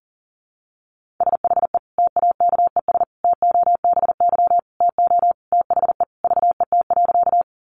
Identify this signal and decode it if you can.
Morse code, heard as a keyed tone, about 40 wpm, 700 Hz. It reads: SHE TAKES TOBY TO THE VET.
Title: Easy Phrases 40wpm